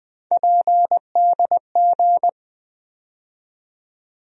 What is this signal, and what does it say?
Morse code, heard as a keyed tone, about 20 wpm, 700 Hz. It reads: PDG
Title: USA Call Sign Postfixes - 3 Letters 20wpm